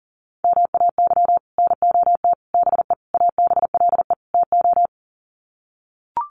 Morse code, heard as a keyed tone, about 40 wpm, 700 Hz. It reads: MAY NOT BE ABLE TO
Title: Sets of 5 Words 40wpm